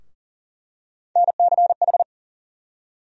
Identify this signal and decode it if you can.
Morse code, heard as a keyed tone, about 40 wpm, 700 Hz. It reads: NCH
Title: USA Call Sign Postfixes - 3 Letters 40wpm